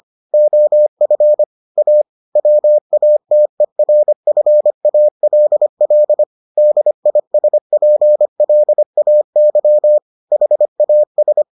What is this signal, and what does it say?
Morse code, heard as a keyed tone, about 25 wpm, 600 Hz. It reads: OF A WATERFALL DISPLAY HAS